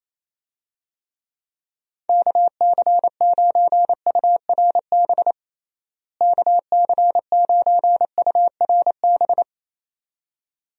Morse code, heard as a keyed tone, about 28 wpm, 700 Hz. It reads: KC9URB KC9URB
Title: USA Call Signs - 6 Characters - Repeated 2x - 28wpm